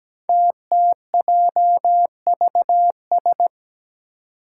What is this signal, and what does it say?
Morse code, heard as a keyed tone, about 17 wpm, 700 Hz. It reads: TTJVS
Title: Five Challenging Letter-Number Combinations 17wpm